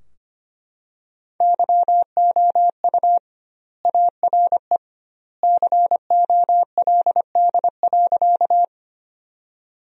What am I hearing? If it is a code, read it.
Morse code, heard as a keyed tone, about 25 wpm, 700 Hz. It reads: YOU ARE COLD.
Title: 2x Word-Spacing - Sentences from Top 400 Words - 25wpm